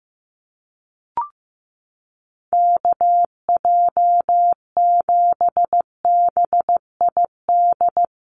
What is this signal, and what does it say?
Morse code, heard as a keyed tone, about 15 wpm, 700 Hz. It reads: KJ7BID
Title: USA Call Signs Repeated 3x 15wpm